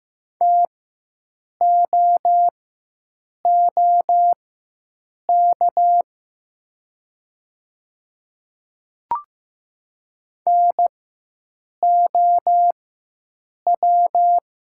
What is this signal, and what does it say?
Morse code, heard as a keyed tone, about 15 wpm, spaced as for 7 wpm, 700 Hz. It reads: TOOK NOW
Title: 4x Character Spacing - Top 300 Words - 15wpm